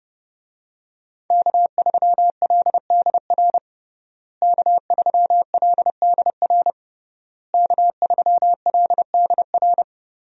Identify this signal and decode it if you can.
Morse code, heard as a keyed tone, about 30 wpm, 700 Hz. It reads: K3LDR K3LDR K3LDR
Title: USA Call Signs - 5 Characters - Repeated 3x - 30wpm